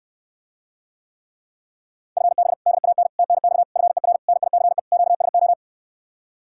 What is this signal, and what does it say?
Morse code, heard as a keyed tone, about 69 wpm, 700 Hz. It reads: FB DAN TNX FER NICE QSO